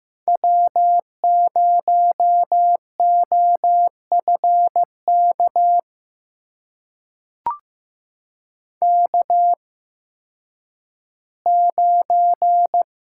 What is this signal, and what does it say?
Morse code, heard as a keyed tone, about 15 wpm, spaced as for 10 wpm, 700 Hz. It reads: W0OFK K9
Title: USA Call Signs - 5 Characters - 8x Character Spacing - 15wpm